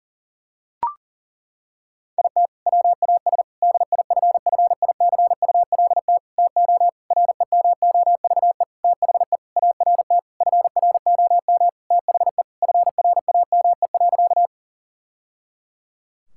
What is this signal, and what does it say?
Morse code, heard as a keyed tone, about 40 wpm, 700 Hz. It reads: IT WAS DIFFICULT TO REMOVE THE ART FROM THE FRAME.
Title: Sentences from Top 1000 Words - Review Part 6 - 40wpm